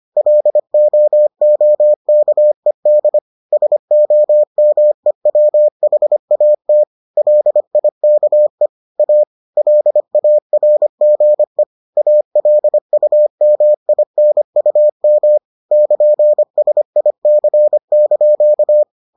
Morse code, heard as a keyed tone, about 25 wpm, 600 Hz. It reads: LOOKED SOMEWHAT LIKE A LARGE ALUMINUM <KN>SIC)